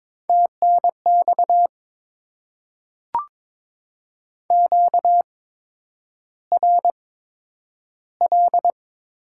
Morse code, heard as a keyed tone, about 22 wpm, spaced as for 9 wpm, 700 Hz. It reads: TNX QRL
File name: Top 20 QSO Elements - 8x Character Spacing - 22wpm